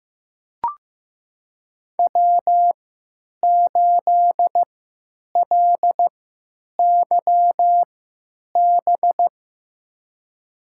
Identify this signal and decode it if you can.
Morse code, heard as a keyed tone, about 15 wpm, spaced as for 9 wpm, 700 Hz. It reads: W8LYB
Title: USA Call Signs - 5 Characters - 3x Character Spacing - 15wpm